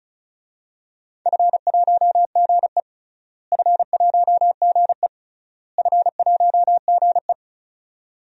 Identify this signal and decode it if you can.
Morse code, heard as a keyed tone, about 34 wpm, 700 Hz. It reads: F1GE F1GE F1GE
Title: DXCC-100 Call Signs - 4 Characters - Repeated 3x - 35wpm